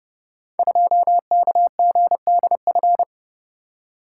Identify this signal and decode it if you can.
Morse code, heard as a keyed tone, about 29 wpm, 690 Hz. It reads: 2KGDF
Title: Code Groups - 5 Letter:Numbers - 30wpm